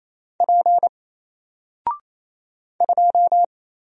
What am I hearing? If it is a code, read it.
Morse code, heard as a keyed tone, about 27 wpm, 700 Hz. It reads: P 2
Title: Single Character (Letter:Number:Punctuation:Prosign) 28wpm